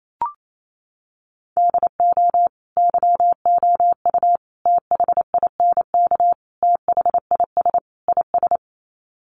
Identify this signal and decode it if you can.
Morse code, heard as a keyed tone, about 28 wpm, 700 Hz. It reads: DO YOU THINK THIS IS
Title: Sets of 5 Words Encore2 28wpm